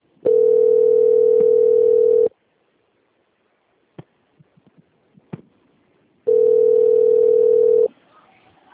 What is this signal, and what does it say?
Ringback tone (North America), 440 + 480 Hz.